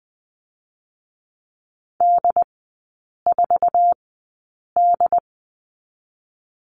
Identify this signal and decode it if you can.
Morse code, heard as a keyed tone, about 20 wpm, spaced as for 8 wpm, 700 Hz. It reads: D4D
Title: 067 Lesson 12e - Letter d - Three Characters 20wpm